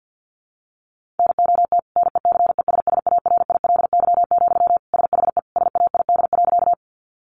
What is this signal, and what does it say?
Morse code, heard as a keyed tone, about 50 wpm, 700 Hz. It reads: NOT NECESSARILY, SHE SAID.
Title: Sentences from Top 1400 Words 50wpm